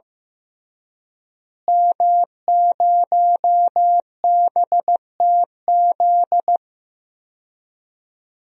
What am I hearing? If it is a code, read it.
Morse code, heard as a keyed tone, about 15 wpm, 700 Hz. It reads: M0BTZ
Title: DXCC-100 Call Signs - 5 Characters - Repeated 3x - 15wpm